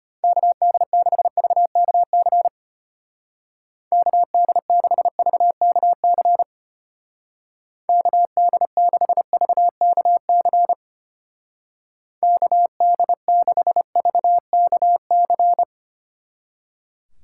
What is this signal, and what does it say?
Morse code, heard as a keyed tone, about 32 wpm, 700 Hz. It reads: KD6VKC KD6VKC KD6VKC KD6VKC